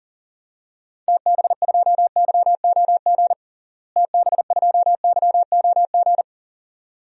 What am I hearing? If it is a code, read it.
Morse code, heard as a keyed tone, about 40 wpm, 700 Hz. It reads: TB2YOG TB2YOG
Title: DXCC-100 Call Signs - 6 Characters - Repeated 2x - 40wpm